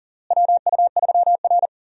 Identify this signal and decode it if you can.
Morse code, heard as a keyed tone, about 40 wpm, 700 Hz. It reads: WU3R
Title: USA Call Signs - 4 Characters - Repeated 2x - 40wpm